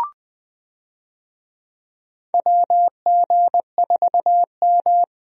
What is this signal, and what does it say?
Morse code, heard as a keyed tone, about 20 wpm, 700 Hz. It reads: WG4M
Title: USA Call Signs - 4 Characters - Repeated 2x - 20wpm